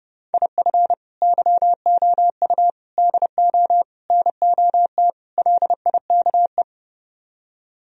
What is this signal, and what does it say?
Morse code, heard as a keyed tone, about 39 wpm, 700 Hz. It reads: IF YOU DO NOT LIKE